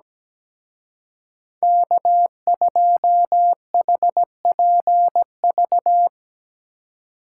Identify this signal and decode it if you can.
Morse code, heard as a keyed tone, about 19 wpm, 700 Hz. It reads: K2HPV
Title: Speed Racing - USA 1x3 Call Signs 17wpm